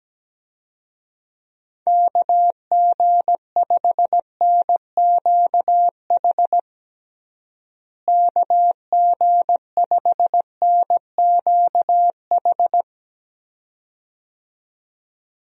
Morse code, heard as a keyed tone, about 17 wpm, 700 Hz. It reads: KG5NQH KG5NQH
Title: USA Call Signs - 6 Characters - Repeated 2x - 17wpm